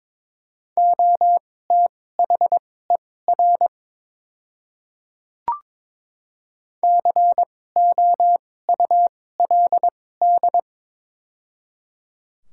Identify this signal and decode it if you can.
Morse code, heard as a keyed tone, about 22 wpm, spaced as for 16 wpm, 700 Hz. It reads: OTHER COULD
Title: 2x Character Spacing - Top 100 Words - 22wpm